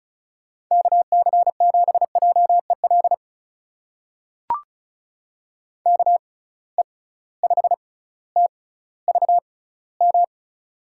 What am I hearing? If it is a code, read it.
Morse code, heard as a keyed tone, about 36 wpm, spaced as for 18 wpm, 700 Hz. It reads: KC7JEL KE5TVM
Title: USA Call Signs - 6 Characters - 6x Character Spacing - 35wpm